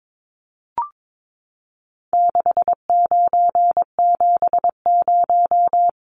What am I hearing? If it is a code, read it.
Morse code, heard as a keyed tone, about 22 wpm, 700 Hz. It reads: 6970
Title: Numbers - 4-Digit 22wpm